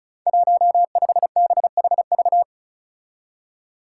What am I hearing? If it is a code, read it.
Morse code, heard as a keyed tone, about 36 wpm, 700 Hz. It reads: 15BHV
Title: Five Challenging Letter-Number Combinations 35wpm